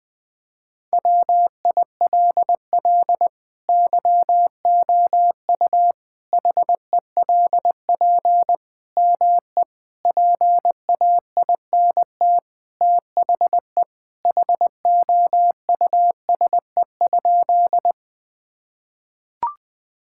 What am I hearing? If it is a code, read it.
Morse code, heard as a keyed tone, about 20 wpm, 700 Hz. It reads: WILL YOU HELP ME PAINT THE HOUSE?
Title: Sentences from Top 1000 Words - Review Part 6 - 20wpm